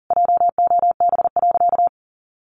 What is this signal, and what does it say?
Morse code, heard as a keyed tone, about 40 wpm, 700 Hz. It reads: JOB.
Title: Sentences from Top 600 Words 40wpm